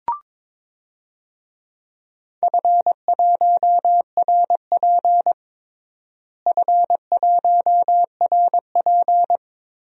Morse code, heard as a keyed tone, about 22 wpm, 700 Hz. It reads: F1RP F1RP
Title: DXCC-100 Call Signs - 4 Characters - Repeated 3x - 22wpm